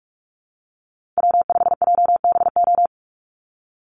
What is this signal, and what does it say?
Morse code, heard as a keyed tone, about 44 wpm, 700 Hz. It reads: W5JBO
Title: Call Signs Repeated 2x - 45wpm